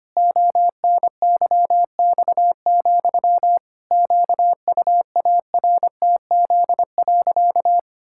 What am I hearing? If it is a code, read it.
Morse code, heard as a keyed tone, about 25 wpm, 700 Hz. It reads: ONYX, QUARTZ.
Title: Pangram Challenge v2 - 25wpm